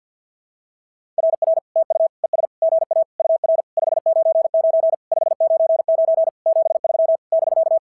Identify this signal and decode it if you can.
Morse code, heard as a keyed tone, about 46 wpm, 660 Hz. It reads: RR TU ES GA UR 599 599 73 <BK>
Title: SOTA QSO Practice 50wpm